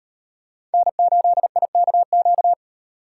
Morse code, heard as a keyed tone, about 38 wpm, 700 Hz. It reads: N8IKQ